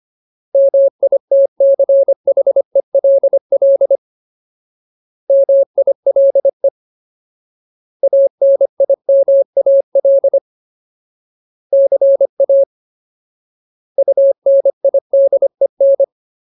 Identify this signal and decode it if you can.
Morse code, heard as a keyed tone, about 25 wpm, 550 Hz. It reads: MITCHELL MILE ANIMAL CA UNIDEN
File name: Ultimate Word List / 25WPM Ultimate